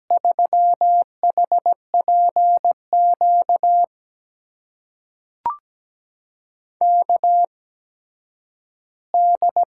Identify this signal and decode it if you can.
Morse code, heard as a keyed tone, about 17 wpm, spaced as for 10 wpm, 700 Hz. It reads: 3HPQ KD
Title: USA Call Signs - 6 Characters - 8x Character Spacing - 17wpm